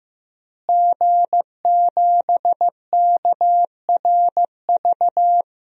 Morse code, heard as a keyed tone, about 15 wpm, 700 Hz. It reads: G7KRV
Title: DXCC-100 Call Signs - 5 Characters - Repeated 2x - 15wpm